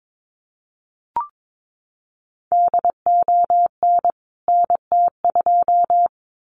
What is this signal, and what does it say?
Morse code, heard as a keyed tone, about 22 wpm, 700 Hz. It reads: DON NT2
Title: CWT Exchange 22wpm